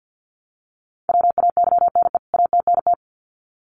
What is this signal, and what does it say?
Morse code, heard as a keyed tone, about 50 wpm, 700 Hz. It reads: WAYNE ATNT
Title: CWT Exchange 50wpm